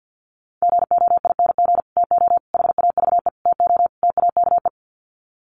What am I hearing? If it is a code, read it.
Morse code, heard as a keyed tone, about 50 wpm, 700 Hz. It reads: GOING TO HAVE TO TAKE
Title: Sets of 5 Words Encore 50wpm